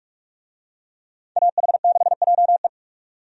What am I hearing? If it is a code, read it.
Morse code, heard as a keyed tone, about 46 wpm, 700 Hz. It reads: AH6JE
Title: USA Call Signs - 5 Characters - 45wpm